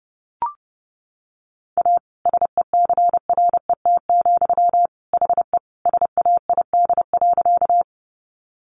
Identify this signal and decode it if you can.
Morse code, heard as a keyed tone, about 30 wpm, 700 Hz. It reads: A SECRET, HE SAID.
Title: Sentences from Top 1200 Words 30wpm